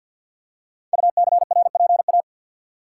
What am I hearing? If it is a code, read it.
Morse code, heard as a keyed tone, about 49 wpm, 700 Hz. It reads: UCRPA
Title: Code Groups - 5 Letter:Numbers - 50wpm